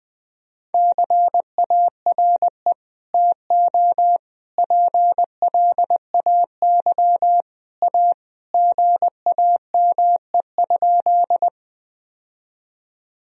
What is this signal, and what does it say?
Morse code, heard as a keyed tone, about 20 wpm, 700 Hz. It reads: CARE TO PLAY A GAME?
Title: Sentences from Top 1000 Words - Review Part 1 - 20wpm